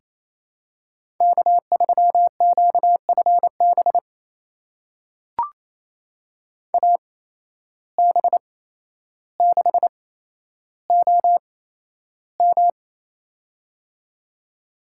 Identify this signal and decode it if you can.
Morse code, heard as a keyed tone, about 27 wpm, spaced as for 12 wpm, 700 Hz. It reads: K3QFB AB6OM
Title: USA Call Signs - 5 Characters - 8x Character Spacing - 28wpm